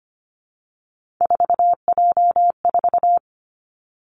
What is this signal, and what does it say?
Morse code, heard as a keyed tone, about 25 wpm, 700 Hz. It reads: 4J4
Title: Three Challenging Letter-Number Combinations 25wpm